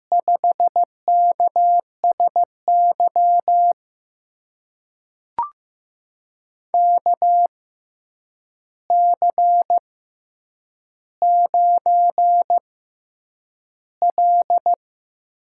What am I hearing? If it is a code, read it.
Morse code, heard as a keyed tone, about 15 wpm, spaced as for 8 wpm, 700 Hz. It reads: 5KSY KC9L
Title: USA Call Signs - 6 Characters - 6x Character Spacing - 15wpm